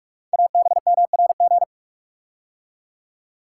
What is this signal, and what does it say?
Morse code, heard as a keyed tone, about 44 wpm, 700 Hz. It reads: ABMRG
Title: Code Groups - 5 Letters - 45wpm